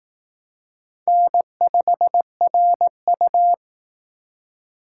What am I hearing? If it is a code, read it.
Morse code, heard as a keyed tone, about 19 wpm, 700 Hz. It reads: N5RU